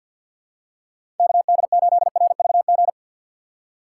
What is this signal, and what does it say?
Morse code, heard as a keyed tone, about 48 wpm, 700 Hz. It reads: KD8RVG
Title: USA Call Signs - 6 Characters - 4x Character Spacing - 50wpm